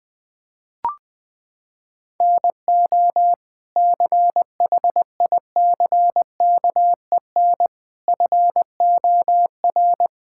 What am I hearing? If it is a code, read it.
Morse code, heard as a keyed tone, about 20 wpm, 700 Hz. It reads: NO CHICKEN FOR
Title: Sentences from Top 1000 Words - Review Part 5 - 20wpm